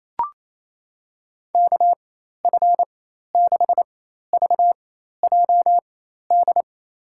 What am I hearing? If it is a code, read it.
Morse code, heard as a keyed tone, about 28 wpm, spaced as for 13 wpm, 700 Hz. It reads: KF6VJD
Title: USA Call Signs - 6 Characters - 4x Character Spacing - 28wpm